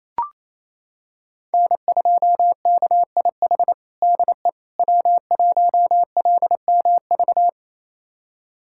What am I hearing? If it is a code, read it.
Morse code, heard as a keyed tone, about 28 wpm, 700 Hz. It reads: N2KIH DE W1LMV
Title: USA Callsign DE Callsign - 5 Characters - 28wpm